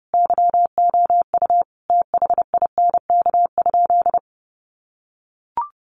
Morse code, heard as a keyed tone, about 30 wpm, 700 Hz. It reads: YOU THINK?
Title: Sentences from Top 200 Words 30wpm